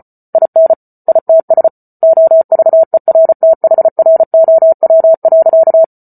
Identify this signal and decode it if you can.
Morse code, heard as a keyed tone, about 34 wpm, 650 Hz. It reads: IN ITS OVERTHROW.